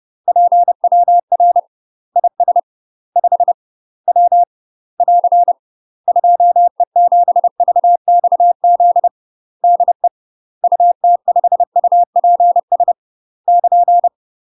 Morse code, heard as a keyed tone, about 30 wpm, 700 Hz. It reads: PWR IS 5 W <AR> 2E7VXZ DE UT5UPS <KN>
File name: UK MASTER QSOs TRAINING / 30WPM / UK MASTER QSOs 0001